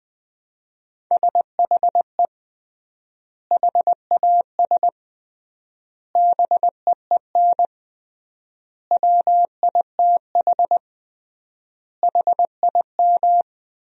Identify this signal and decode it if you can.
Morse code, heard as a keyed tone, about 20 wpm, 700 Hz. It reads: SHE HAS BEEN WITH HIM
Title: 3x Word-Spacing - Sentences from Top 200 Words - 20wpm